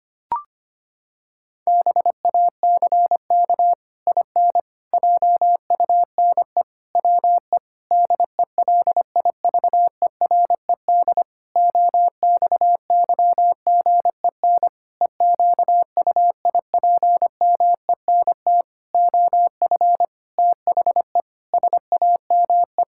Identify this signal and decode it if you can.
Morse code, heard as a keyed tone, about 25 wpm, 700 Hz. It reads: BACK IN JUNE WE DELIVERED OXYGEN EQUIPMENT OF THE SAME
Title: Pangram Challenge v2 - 25wpm